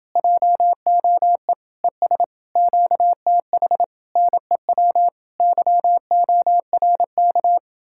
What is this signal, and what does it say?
Morse code, heard as a keyed tone, about 29 wpm, 700 Hz. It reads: JOE ES QTH NEW YORK